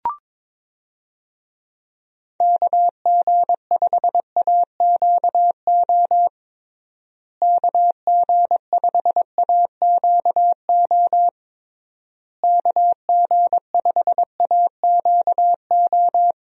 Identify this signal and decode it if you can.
Morse code, heard as a keyed tone, about 22 wpm, 700 Hz. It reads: KG5AQO KG5AQO KG5AQO
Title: USA Call Signs - 6 Characters - Repeated 3x - 22wpm